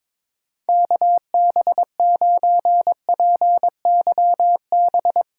Morse code, heard as a keyed tone, about 22 wpm, 700 Hz. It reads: KB9PYB
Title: USA Callsign DE Callsign 22wpm